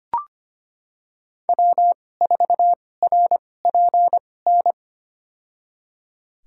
Morse code, heard as a keyed tone, about 25 wpm, spaced as for 18 wpm, 700 Hz. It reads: W4RPN
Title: USA Call Signs - 2x Character Spacing - 25wpm